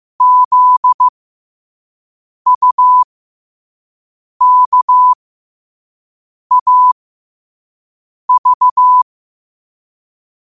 Morse code, heard as a keyed tone, about 15 wpm, spaced as for 5 wpm, 1000 Hz. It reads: ZUKAV
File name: qotdfiles / 5wpm